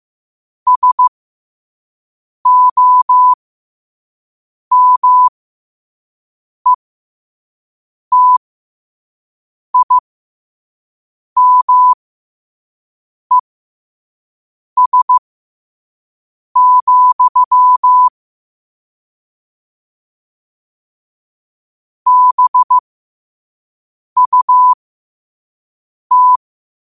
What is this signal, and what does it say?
Morse code, heard as a keyed tone, about 15 wpm, spaced as for 5 wpm, 1000 Hz. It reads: SOMETIMES, BUT
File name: qotdfiles / 5wpm